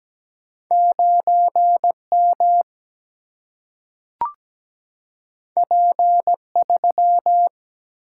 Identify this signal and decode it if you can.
Morse code, heard as a keyed tone, about 17 wpm, 700 Hz. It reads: 9M P3
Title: Single-Letter Single-Number 17wpm